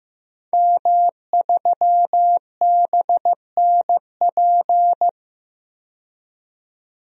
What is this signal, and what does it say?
Morse code, heard as a keyed tone, about 15 wpm, 700 Hz. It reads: M3BNP
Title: DXCC-100 Call Signs - 5 Characters - 15wpm